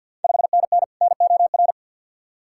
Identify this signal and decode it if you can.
Morse code, heard as a keyed tone, about 48 wpm, 700 Hz. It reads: 5NN NOR
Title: Ontario QSO Party Exchange 50wpm